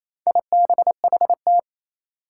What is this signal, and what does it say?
Morse code, heard as a keyed tone, about 28 wpm, 700 Hz. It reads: IBHT
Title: Four Challenging Letter-Number Combinations 28wpm